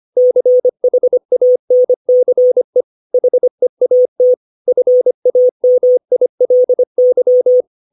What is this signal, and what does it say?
Morse code, heard as a keyed tone, about 25 wpm, 500 Hz. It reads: CHANCE HEAT FAMILY
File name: 632 words / 632 Words 25WPM